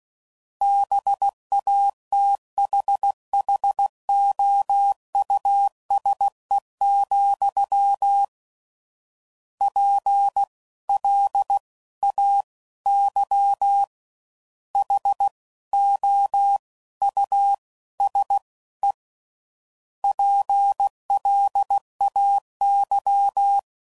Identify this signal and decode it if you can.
Morse code, heard as a keyed tone, about 16 wpm, 800 Hz. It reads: BATHHOUSE, P L A Y H O U S E PLAY
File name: Suffixes / General / Houses